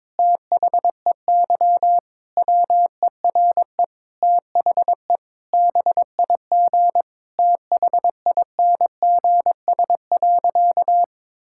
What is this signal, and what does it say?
Morse code, heard as a keyed tone, about 22 wpm, 700 Hz. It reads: THEY WERE THE BIG THINGS.
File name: Difficult Sentences with Hints - 22wpm